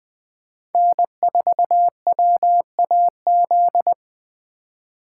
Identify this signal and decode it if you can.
Morse code, heard as a keyed tone, about 21 wpm, 700 Hz. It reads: N4WAZ
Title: USA Call Signs - 5 Characters - 2x Character Spacing - 20wpm